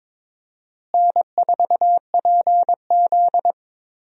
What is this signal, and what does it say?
Morse code, heard as a keyed tone, about 22 wpm, 700 Hz. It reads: N4PZ
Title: USA Call Signs - 4 Characters - Repeated 2x - 22wpm